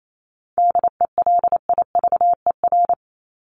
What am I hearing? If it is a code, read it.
Morse code, heard as a keyed tone, about 28 wpm, 700 Hz. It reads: DELIVER
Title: Top 1751-2000 Words 28wpm